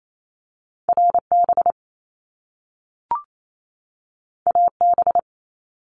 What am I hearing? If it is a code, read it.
Morse code, heard as a keyed tone, about 28 wpm, 700 Hz. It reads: RB AB
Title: Top Two Letter Combinations in Words 28wpm